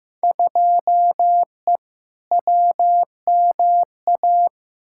Morse code, heard as a keyed tone, about 15 wpm, 700 Hz. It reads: 2E WMA
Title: Field Day Exchanges Encore - 15wpm